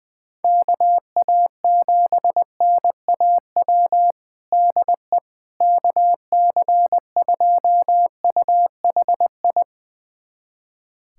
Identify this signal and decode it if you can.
Morse code, heard as a keyed tone, about 20 wpm, 700 Hz. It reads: KA7NAW DE KC2UHI
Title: USA Callsign DE Callsign 20wpm